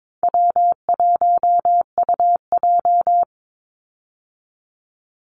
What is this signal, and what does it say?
Morse code, heard as a keyed tone, about 22 wpm, 700 Hz. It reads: W1UJ
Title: Easy Call Signs - 22wpm